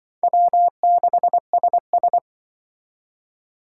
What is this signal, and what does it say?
Morse code, heard as a keyed tone, about 24 wpm, 700 Hz. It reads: W6SS
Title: Speed Racing - USA 1x2 Call Signs 20wpm